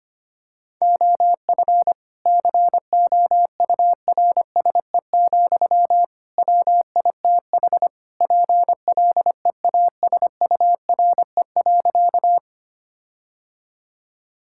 Morse code, heard as a keyed tone, about 25 wpm, 700 Hz. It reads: OF COURSE, WITH PLEASURE.
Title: Sentences from Top 1600 Words 25wpm